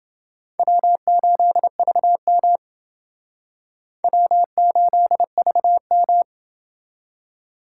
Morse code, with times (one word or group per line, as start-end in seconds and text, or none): 0.6-2.6 W8VM
4.0-6.2 W8VM